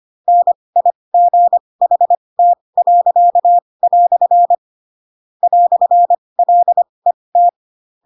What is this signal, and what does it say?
Morse code, heard as a keyed tone, about 25 wpm, spaced as for 20 wpm, 700 Hz. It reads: NIGHT." "LET
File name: Pinocchio / Pinocchio ch4 15 25 ik1hge